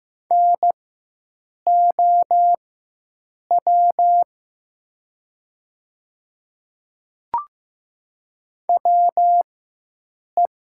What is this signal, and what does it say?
Morse code, heard as a keyed tone, about 15 wpm, spaced as for 7 wpm, 700 Hz. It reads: NOW WE
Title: 4x Character Spacing - Top 100 Words - 15wpm